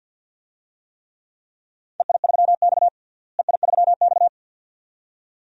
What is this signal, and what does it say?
Morse code, heard as a keyed tone, about 53 wpm, 700 Hz. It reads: EI3X EI3X
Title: DXCC-100 Call Signs - 4 Characters - Repeated 2x - 50wpm